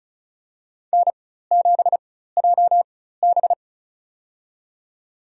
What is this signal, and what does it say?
Morse code, heard as a keyed tone, about 35 wpm, spaced as for 16 wpm, 700 Hz. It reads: N7JB
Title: USA Call Signs - 4 Characters - 4x Character Spacing - 35wpm